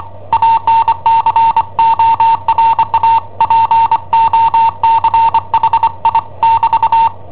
Morse code, heard as a keyed tone, about 25 wpm, 910 Hz. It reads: ANCORA POCHI <BT>
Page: Cassetta esercizi preparata da I7OHP per IW5ELP